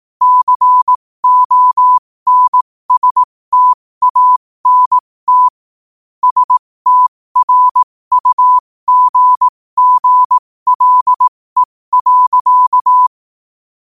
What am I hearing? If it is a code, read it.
Morse code, heard as a keyed tone, about 18 wpm, spaced as for 15 wpm, 1000 Hz. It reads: CONSTANT STRUGGLE.
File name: qotdfiles / 15wpm